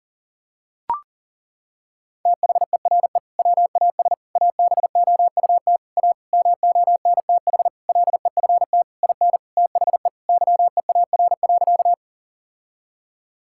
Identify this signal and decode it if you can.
Morse code, heard as a keyed tone, about 40 wpm, 700 Hz. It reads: THERE WAS ABOUT A MONTH LEFT IN THE YEAR.
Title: Sentences from Top 1000 Words - Review Part 5 - 40wpm